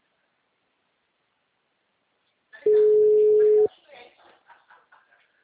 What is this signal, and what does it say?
Ringback tone (Europe, 425 Hz), with speech.
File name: out- -2050- - - .35298